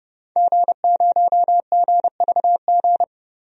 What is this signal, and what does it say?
Morse code, heard as a keyed tone, about 30 wpm, 700 Hz. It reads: G0GVG
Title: DXCC-100 Call Signs - 5 Characters - 30wpm